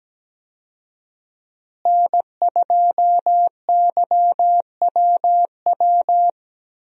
Morse code, heard as a keyed tone, about 17 wpm, 700 Hz. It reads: N2YWW
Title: USA Call Signs - 5 Characters - Repeated 2x - 17wpm